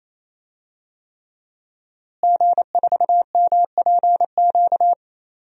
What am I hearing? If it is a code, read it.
Morse code, heard as a keyed tone, about 28 wpm, 700 Hz. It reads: G4MPQ
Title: DXCC-100 Call Signs - 5 Characters - Repeated 3x - 28wpm